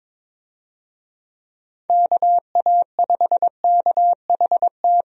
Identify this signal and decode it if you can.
Morse code, heard as a keyed tone, about 22 wpm, 700 Hz. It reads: KA5KHT
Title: USA Call Signs - 6 Characters - Repeated 3x - 22wpm